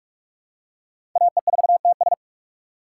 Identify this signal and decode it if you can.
Morse code, heard as a keyed tone, about 45 wpm, 700 Hz. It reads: AE4TS
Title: USA Call Signs - 5 Characters - 2x Character Spacing - 45wpm